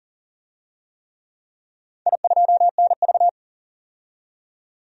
Morse code, heard as a keyed tone, about 40 wpm, 700 Hz. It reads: I2NV
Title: DXCC-100 Call Signs - 4 Characters - 40wpm